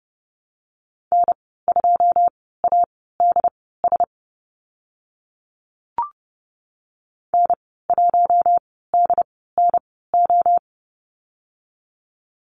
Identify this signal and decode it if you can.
Morse code, heard as a keyed tone, about 29 wpm, spaced as for 17 wpm, 700 Hz. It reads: N2ADS N1DNO
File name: Lesson 15g - Number 5 - Call Signs 30wpm